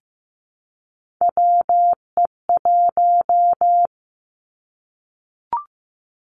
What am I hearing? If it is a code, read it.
Morse code, heard as a keyed tone, about 15 wpm, 700 Hz. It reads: WE1
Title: Three Challenging Letter-Number Combinations 15wpm